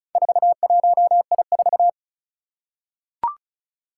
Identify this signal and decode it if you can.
Morse code, heard as a keyed tone, about 36 wpm, 700 Hz. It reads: 41I4
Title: Four Challenging Letter-Number Combinations 35wpm